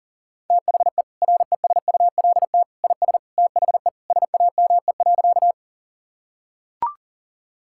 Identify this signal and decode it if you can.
Morse code, heard as a keyed tone, about 37 wpm, 700 Hz. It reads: THE RESULT IS THE SAME.
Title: Sentences from Top 1000 Words - Review Part 5 - 40wpm